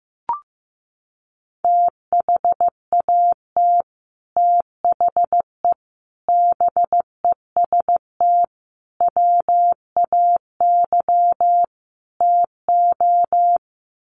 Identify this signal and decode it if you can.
Morse code, heard as a keyed tone, about 15 wpm, 700 Hz. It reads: THAT THE BEST WAY TO
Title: Sets of 5 Words Encore2 15wpm